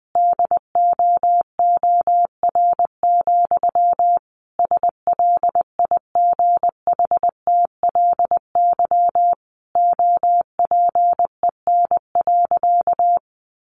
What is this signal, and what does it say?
Morse code, heard as a keyed tone, about 20 wpm, 700 Hz. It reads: DOOR, SLIGHTLY OPEN.
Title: Sentences from Top 1300 Words 20wpm